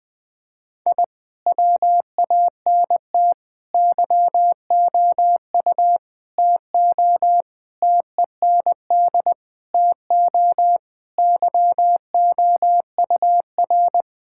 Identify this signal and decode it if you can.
Morse code, heard as a keyed tone, about 20 wpm, 700 Hz. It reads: I WANT YOU TO TEND TO YOUR
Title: Sentences from Top 1000 Words - Review Part 5 - 20wpm